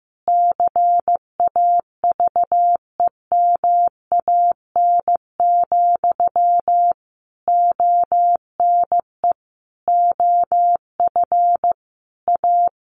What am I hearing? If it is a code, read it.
Morse code, heard as a keyed tone, about 15 wpm, 700 Hz. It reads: CAVEMAN, ONE OF A
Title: A Sound Of Thunder 15wpm